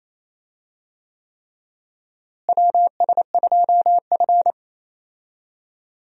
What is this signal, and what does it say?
Morse code, heard as a keyed tone, about 28 wpm, 700 Hz. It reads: WS2F
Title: USA Call Signs - 4 Characters - Repeated 3x - 28wpm